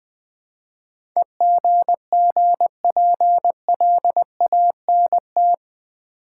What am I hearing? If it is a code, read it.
Morse code, heard as a keyed tone, about 20 wpm, 700 Hz. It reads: EGGPLANT
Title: Easy Words with Hints - 20wpm